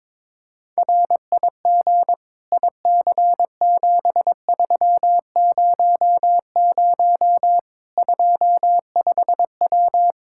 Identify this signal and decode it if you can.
Morse code, heard as a keyed tone, about 22 wpm, 700 Hz. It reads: RIG IC7300 25W
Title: Practice QSO - Exchange 2 - Info - 22wpm